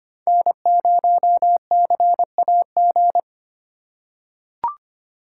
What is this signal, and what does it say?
Morse code, heard as a keyed tone, about 25 wpm, 700 Hz. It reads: N0CAG
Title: USA Call Signs - 5 Characters - 3x Character Spacing - 25wpm